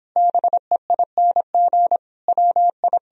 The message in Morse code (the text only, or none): BEING WI